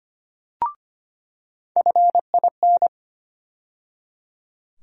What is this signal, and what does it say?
Morse code, heard as a keyed tone, about 25 wpm, 700 Hz. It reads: FIN
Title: USA Call Sign Postfixes - 3 Letters 25wpm